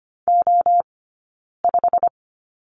Morse code, heard as a keyed tone, about 25 wpm, spaced as for 9 wpm, 690 Hz. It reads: O5
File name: Lesson 15d - Number 5 - Two Characters 25wpm